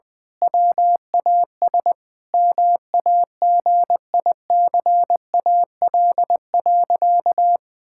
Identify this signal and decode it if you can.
Morse code, heard as a keyed tone, about 20 wpm, 700 Hz. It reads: WAS MAGICAL.
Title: Sentences with 5-Word Phrases - 20wpm